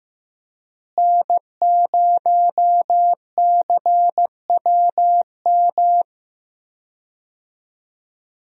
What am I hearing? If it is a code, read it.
Morse code, heard as a keyed tone, about 15 wpm, 700 Hz. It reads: N0CWM
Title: USA Call Signs - 5 Characters - Repeated 3x - 15wpm